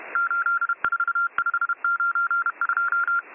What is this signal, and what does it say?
Morse code, heard as a keyed tone, about 31 wpm, 1390 Hz. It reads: 84592